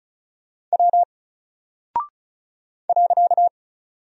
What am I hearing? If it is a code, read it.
Morse code, heard as a keyed tone, about 36 wpm, 700 Hz. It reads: W .
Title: Single Character (Letter:Number:Punctuation:Prosign) 35wpm